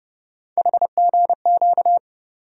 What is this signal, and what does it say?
Morse code, heard as a keyed tone, about 30 wpm, 700 Hz. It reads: HGQ
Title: USA Call Sign Postfixes - 3 Letters 30wpm